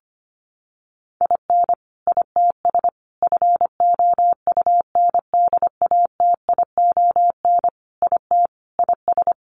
Morse code, heard as a keyed tone, about 25 wpm, 700 Hz. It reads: IN ITS FOUNDATION IT IS
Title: Sentences from Top 1200 Words 25wpm